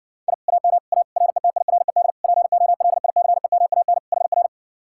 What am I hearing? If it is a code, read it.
Morse code, heard as a keyed tone, about 70 wpm, 700 Hz. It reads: I AM A RETIRED POLICEMAN HR